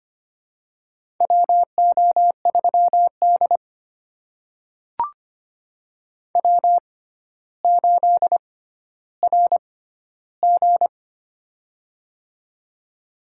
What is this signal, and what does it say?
Morse code, heard as a keyed tone, about 25 wpm, spaced as for 13 wpm, 700 Hz. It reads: WO3D W8RG
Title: USA Call Signs - 4 Characters - 6x Character Spacing - 25wpm